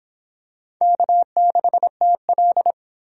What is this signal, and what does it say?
Morse code, heard as a keyed tone, about 25 wpm, 700 Hz. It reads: K6TL